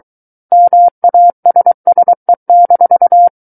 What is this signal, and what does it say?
Morse code, heard as a keyed tone, about 23 wpm, 700 Hz. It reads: MASSE-
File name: Book One / 14-Chapter 14 - 23 WPM